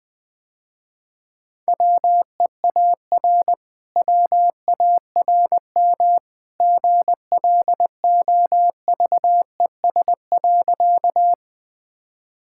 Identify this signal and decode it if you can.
Morse code, heard as a keyed tone, about 20 wpm, 700 Hz. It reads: WEAR WARM GLOVES.
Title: Easy Sentences with Hints - 20wpm